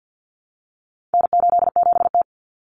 Morse code, heard as a keyed tone, about 50 wpm, 700 Hz. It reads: N87T
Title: Common Cut Numbers - 4-Digit 50wpm